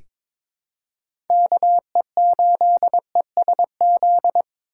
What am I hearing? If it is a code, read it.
Morse code, heard as a keyed tone, about 22 wpm, 700 Hz. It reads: KE8ESZ
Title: USA Call Signs - 6 Characters - 4x Character Spacing - 22wpm